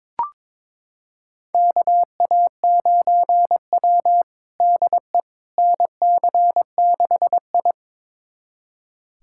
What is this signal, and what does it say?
Morse code, heard as a keyed tone, about 23 wpm, 700 Hz. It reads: KA9W DE NC6I
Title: USA Callsign DE Callsign - 4 Characters - 22wpm